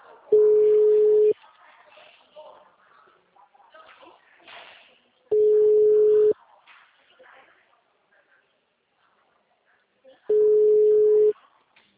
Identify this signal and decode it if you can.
Ringback tone (Europe, 425 Hz), with speech.
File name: out- -2010- - - .2571